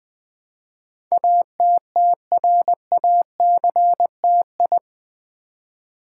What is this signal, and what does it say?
Morse code, heard as a keyed tone, about 20 wpm, 700 Hz. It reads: ATTRACTI
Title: Build A Word - 4 Letters At A Time - 10-Letter Words - 20wpm